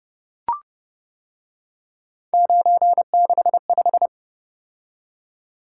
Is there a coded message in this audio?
Yes: Morse code from 2.3 to 4.1 s, heard as a keyed tone, about 29 wpm, 700 Hz. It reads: 965